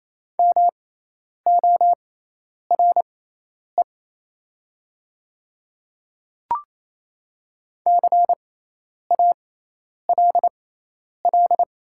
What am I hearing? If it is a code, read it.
Morse code, heard as a keyed tone, about 28 wpm, spaced as for 10 wpm, 690 Hz. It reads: MORE CALL
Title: 6x Character Spacing - Top 100 Words - 28wpm